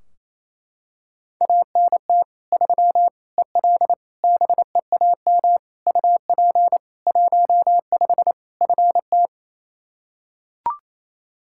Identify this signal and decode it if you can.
Morse code, heard as a keyed tone, about 28 wpm, 700 Hz. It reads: ANT 3 EL BEAM UP 15 FT
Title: Practice QSO - Exchange 2 - Info - 28wpm